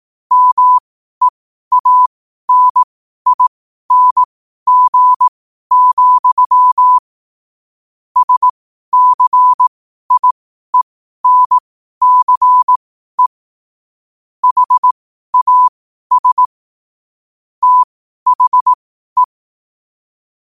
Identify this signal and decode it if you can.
Morse code, heard as a keyed tone, about 18 wpm, spaced as for 12 wpm, 1000 Hz. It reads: MEANING, SCIENCE HAS THE
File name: qotdfiles / 12wpm